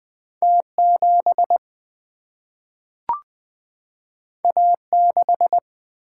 Morse code, heard as a keyed tone, about 20 wpm, 700 Hz. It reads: T7 A6
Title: Single-Letter Single-Number 20wpm